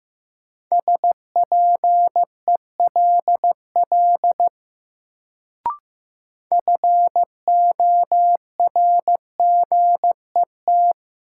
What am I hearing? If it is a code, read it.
Morse code, heard as a keyed tone, about 15 wpm, 700 Hz. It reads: SPELL FORGET
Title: Sentences from Top 1200 Words - New Word Drill Alt - 15wpm